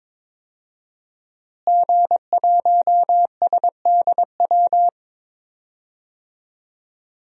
Morse code, heard as a keyed tone, about 22 wpm, 700 Hz. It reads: G1SDW
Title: DXCC-100 Call Signs - 5 Characters - Repeated 3x - 22wpm